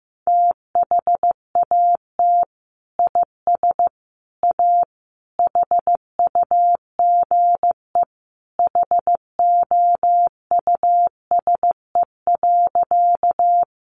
Morse code, heard as a keyed tone, about 15 wpm, 700 Hz. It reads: THAT IS A HUGE HOUSE.
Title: Sentences from Top 1100 Words 15wpm